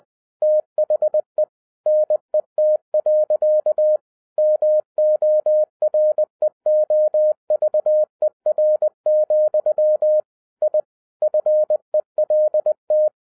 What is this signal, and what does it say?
Morse code, heard as a keyed tone, about 20 wpm, 610 Hz. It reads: THE NET. MOREOVER, I FELT
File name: 2012-01-mp3 / n16-THROUGH EIGHT MONTHS-20wpm